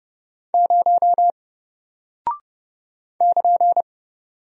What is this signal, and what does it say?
Morse code, heard as a keyed tone, about 30 wpm, 700 Hz. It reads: 0 <KN>
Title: Single Character (Letter:Number:Punctuation:Prosign) Alt 30wpm